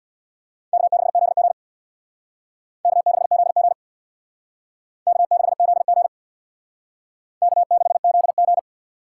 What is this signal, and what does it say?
Morse code, heard as a keyed tone, about 61 wpm, 700 Hz. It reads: K6ZG K6ZG K6ZG K6ZG
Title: Speed Racing - USA 1x2 Call Signs 50wpm